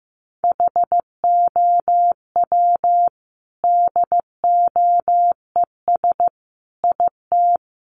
Morse code, heard as a keyed tone, about 15 wpm, 700 Hz. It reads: HOW DOES IT
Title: Sentences from Top 1100 Words 15wpm